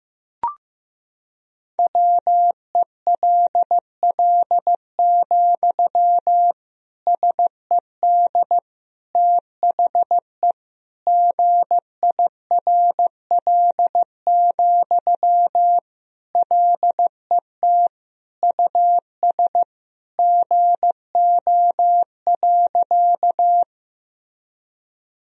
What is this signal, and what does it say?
Morse code, heard as a keyed tone, about 15 wpm, 700 Hz. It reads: WELL, SED THE GIRL, LET US GO.
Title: Sentences from Top 400 Words and Abbreviations 15wpm